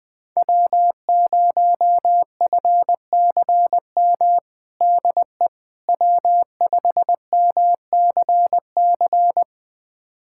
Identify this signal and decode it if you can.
Morse code, heard as a keyed tone, about 20 wpm, 700 Hz. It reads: W0FCM DE W5MCC
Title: USA Callsign DE Callsign - 5 Characters - 20wpm